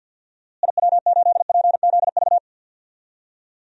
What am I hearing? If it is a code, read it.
Morse code, heard as a keyed tone, about 50 wpm, 700 Hz. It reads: IW8PZV
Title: DXCC-100 Call Signs - 6 Characters - Repeated 3x - 50wpm